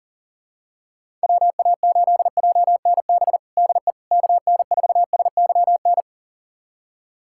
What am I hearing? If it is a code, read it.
Morse code, heard as a keyed tone, about 40 wpm, 700 Hz. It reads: WA8JNB DE KN4SYN
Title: USA Callsign DE Callsign - 6 Characters - 40wpm